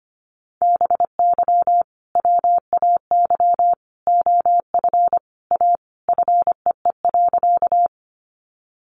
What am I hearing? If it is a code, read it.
Morse code, heard as a keyed tone, about 25 wpm, 700 Hz. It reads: BY WAY OF A FEE.
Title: Sentences from Top 1500 Words 25wpm